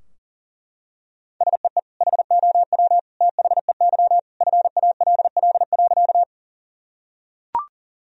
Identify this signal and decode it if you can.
Morse code, heard as a keyed tone, about 40 wpm, 700 Hz. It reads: SEE HOW THEY FALL.
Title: Sentences from Top 1000 Words - Review Part 2 - 40wpm